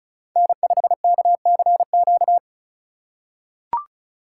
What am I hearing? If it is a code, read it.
Morse code, heard as a keyed tone, about 36 wpm, 700 Hz. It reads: N5KCQ
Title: USA Call Signs - 5 Characters - 2x Character Spacing - 35wpm